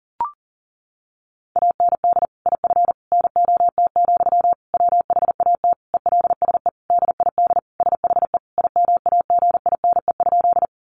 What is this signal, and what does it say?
Morse code, heard as a keyed tone, about 40 wpm, 700 Hz. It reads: AND IF NOT, WHAT ELSE DID SHE IMAGINE?
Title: Sentences from Top 1400 Words 40wpm